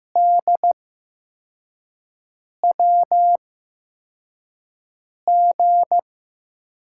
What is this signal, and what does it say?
Morse code, heard as a keyed tone, about 15 wpm, spaced as for 4 wpm, 700 Hz. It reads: DWG
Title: USA Call Signs - 6 Characters - 8x Character Spacing - 15wpm